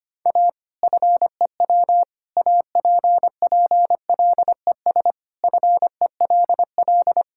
Morse code, heard as a keyed tone, about 25 wpm, 700 Hz. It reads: A FEW APPLES FELL
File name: practice / 25wpm / Sentences with 2-Word Phrases - 25wpm